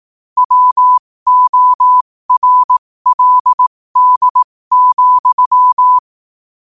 Morse code, heard as a keyed tone, about 18 wpm, spaced as for 15 wpm, 1000 Hz. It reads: WORLD,
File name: qotdfiles / 15wpm